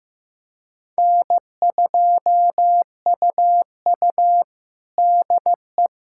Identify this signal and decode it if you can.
Morse code, heard as a keyed tone, about 15 wpm, 700 Hz. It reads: N2UU DE
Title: USA Callsign DE Callsign - 4 Characters - 15wpm